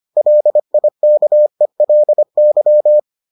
Morse code, heard as a keyed tone, about 25 wpm, 600 Hz. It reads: LIKELY